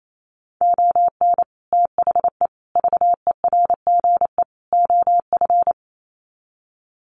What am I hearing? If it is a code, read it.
Morse code, heard as a keyed tone, about 28 wpm, 700 Hz. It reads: ON THE VERGE OF
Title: Sets of 4 Words 28wpm